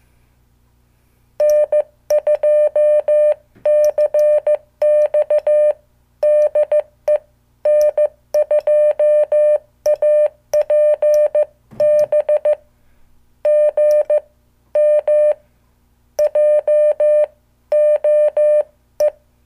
Morse code, heard as a keyed tone, about 15 wpm, spaced as for 13 wpm, 600 Hz. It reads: N2CX DE N2APB GM JOE
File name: morse / qso